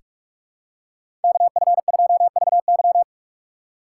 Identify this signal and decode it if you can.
Morse code, heard as a keyed tone, about 46 wpm, 700 Hz. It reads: KF2VY
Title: USA Call Signs - 6x Character Spacing - 45wpm